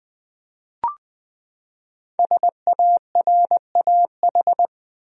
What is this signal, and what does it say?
Morse code, heard as a keyed tone, about 20 wpm, 700 Hz. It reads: SARAH
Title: USA Common Names - Max 5 Letters - 20wpm